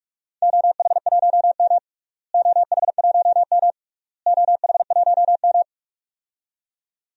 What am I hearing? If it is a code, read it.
Morse code, heard as a keyed tone, about 46 wpm, 700 Hz. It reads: OH1M OH1M OH1M
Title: DXCC-100 Call Signs - 4 Characters - Repeated 3x - 45wpm